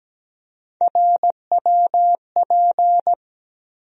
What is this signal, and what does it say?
Morse code, heard as a keyed tone, about 17 wpm, 700 Hz. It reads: RWP
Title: USA Call Sign Postfixes - 3 Letters 17wpm